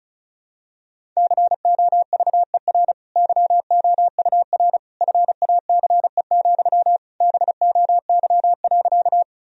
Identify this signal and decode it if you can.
Morse code, heard as a keyed tone, about 36 wpm, 700 Hz. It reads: COVER YOUR FACE, BOY.
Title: Sentences from Top 1000 Words - Review Part 1 - 35wpm